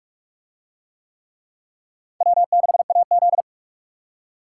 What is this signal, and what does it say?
Morse code, heard as a keyed tone, about 44 wpm, 700 Hz. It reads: W6AZ